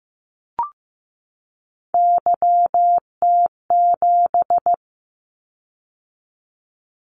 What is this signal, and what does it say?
Morse code, heard as a keyed tone, about 15 wpm, 700 Hz. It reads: YT7
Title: Callsign Prefixes for Top 25 DXCC - 15wpm